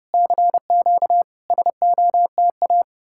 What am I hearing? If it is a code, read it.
Morse code, heard as a keyed tone, about 34 wpm, 700 Hz. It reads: CQ SOTA